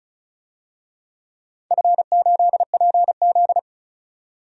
Morse code, heard as a keyed tone, about 34 wpm, 700 Hz. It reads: F8PZ
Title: DXCC-100 Call Signs - 4 Characters - 35wpm